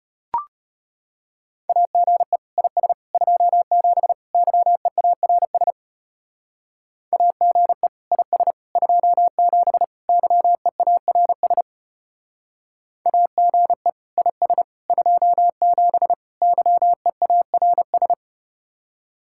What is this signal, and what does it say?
Morse code, heard as a keyed tone, about 33 wpm, 690 Hz. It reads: AGE IS 27 YEARS AGE IS 27 YEARS AGE IS 27 YEARS